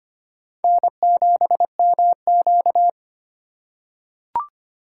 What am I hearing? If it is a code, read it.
Morse code, heard as a keyed tone, about 25 wpm, 700 Hz. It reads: N7MQ
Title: USA Call Signs - 2x Character Spacing - 25wpm